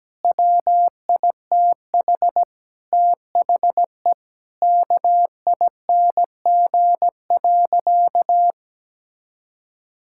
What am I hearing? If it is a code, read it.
Morse code, heard as a keyed tone, about 17 wpm, 700 Hz. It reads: WITH THE KING.
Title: Sentences from Top 1000 Words - Review Part 6 - 17wpm